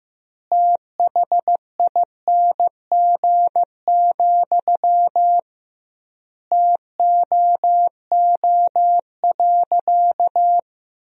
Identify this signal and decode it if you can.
Morse code, heard as a keyed tone, about 15 wpm, 700 Hz. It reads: THING, TOO.
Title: 2x Word-Spacing - Sentences from Top 200 Words - 15wpm